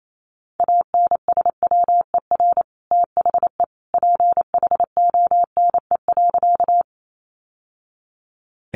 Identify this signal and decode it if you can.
Morse code, heard as a keyed tone, about 28 wpm, 700 Hz. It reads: ANSWER THE PHONE.
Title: Sentences from Top 1100 Words 28wpm